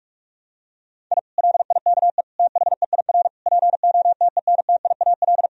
Morse code, heard as a keyed tone, about 46 wpm, 700 Hz. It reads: I LIKE THEIR POTENTIAL
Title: Sentences from Top 1000 Words - Review Part 4 - 45wpm